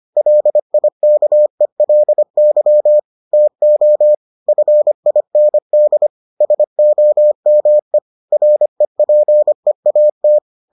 Morse code, heard as a keyed tone, about 25 wpm, 600 Hz. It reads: LIKELY TO FIND SOME REPEAT